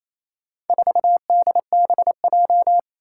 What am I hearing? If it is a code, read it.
Morse code, heard as a keyed tone, about 28 wpm, 700 Hz. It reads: 4DBJ
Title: Four Challenging Letter-Number Combinations 28wpm